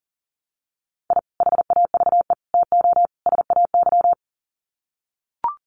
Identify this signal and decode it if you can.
Morse code, heard as a keyed tone, about 40 wpm, 700 Hz. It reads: I HAVE TO SAY
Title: Sets of 4 Words 40wpm